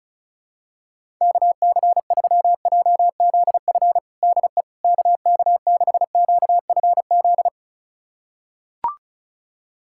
Morse code, heard as a keyed tone, about 36 wpm, 700 Hz. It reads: KC3JZF DE KK6QFZ
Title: USA Callsign DE Callsign 35wpm